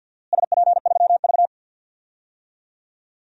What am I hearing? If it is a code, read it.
Morse code, heard as a keyed tone, about 52 wpm, 700 Hz. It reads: SP3V
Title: DXCC-100 Call Signs - 4 Characters - Repeated 3x - 50wpm